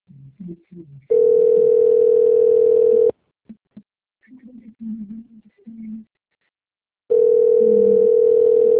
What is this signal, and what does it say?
Ringback tone (North America), 440 + 480 Hz, with speech.
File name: out- - - - - .1666